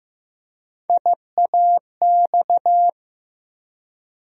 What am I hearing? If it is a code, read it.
Morse code, heard as a keyed tone, about 15 wpm, 700 Hz. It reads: IAX
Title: USA Call Sign Postfixes - 3 Letters 15wpm